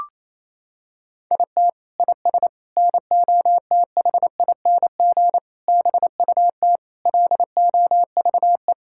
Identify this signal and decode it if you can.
Morse code, heard as a keyed tone, about 28 wpm, 700 Hz. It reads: IT IS NOTHING BUT LOVE
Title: Sentences from Top 1600 Words 28wpm